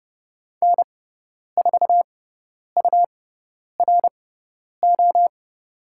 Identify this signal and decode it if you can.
Morse code, heard as a keyed tone, about 30 wpm, spaced as for 10 wpm, 700 Hz. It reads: N4URO
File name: Lesson 17g - Letter u - Call Signs 30wpm